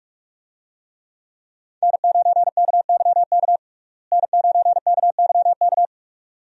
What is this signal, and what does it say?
Morse code, heard as a keyed tone, about 46 wpm, 700 Hz. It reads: N9KYK N9KYK
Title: USA Call Signs - 5 Characters - Repeated 2x - 45wpm